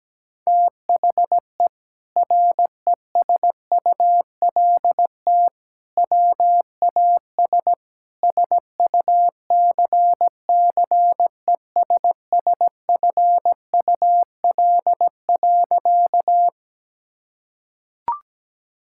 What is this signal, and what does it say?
Morse code, heard as a keyed tone, about 17 wpm, 700 Hz. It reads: THE RESULT WAS SUCCESSFUL.
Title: Sentences from Top 1000 Words - Review Part 6 - 17wpm